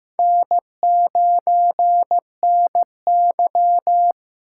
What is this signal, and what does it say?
Morse code, heard as a keyed tone, about 15 wpm, 700 Hz. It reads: N9NY
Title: USA Call Signs - 6x Character Spacing - 15wpm